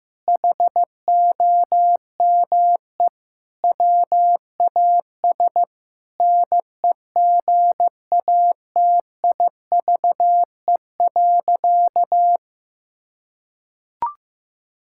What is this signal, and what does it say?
Morse code, heard as a keyed tone, about 16 wpm, 700 Hz. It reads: HOME WAS NEGATIVE.
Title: Sentences from Top 1000 Words - Review Part 6 - 15wpm